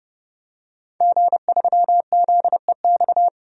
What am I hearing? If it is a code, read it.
Morse code, heard as a keyed tone, about 30 wpm, 700 Hz. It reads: G3ZEX
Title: DXCC-100 Call Signs - 5 Characters - 30wpm